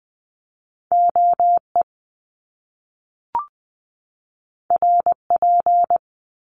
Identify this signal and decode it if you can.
Morse code, heard as a keyed tone, about 20 wpm, 700 Hz. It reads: O E R P
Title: Top Two Letter Combinations in Words 20wpm